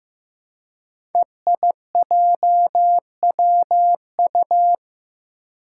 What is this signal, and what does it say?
Morse code, heard as a keyed tone, about 15 wpm, 700 Hz. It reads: EIJWU
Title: Five Challenging Letter-Number Combinations 15wpm